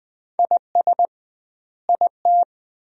Morse code, heard as a keyed tone, about 20 wpm, 700 Hz. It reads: IS IT
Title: 2x Word-Spacing - Sentences from Top 200 Words - 20wpm